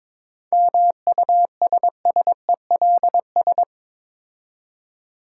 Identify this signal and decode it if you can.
Morse code, heard as a keyed tone, about 22 wpm, 700 Hz. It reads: MUSSELS
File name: Difficult Words with Hints - 22wpm